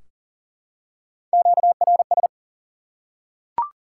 Morse code, heard as a keyed tone, about 40 wpm, 700 Hz. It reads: QRS
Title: Top 90 QSO Elements 40wpm